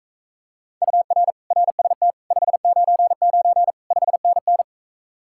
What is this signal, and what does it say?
Morse code, heard as a keyed tone, about 45 wpm, 700 Hz. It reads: UR RST 599 5NN